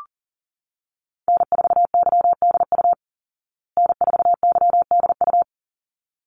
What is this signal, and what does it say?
Morse code, heard as a keyed tone, about 40 wpm, 700 Hz. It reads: N4YDU N4YDU
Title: Call Signs Repeated 3x - 40wpm